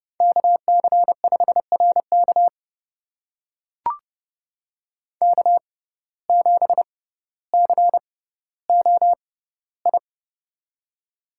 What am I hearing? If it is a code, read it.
Morse code, heard as a keyed tone, about 29 wpm, spaced as for 15 wpm, 700 Hz. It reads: KC5RK K7COI
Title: USA Call Signs - 5 Characters - 6x Character Spacing - 30wpm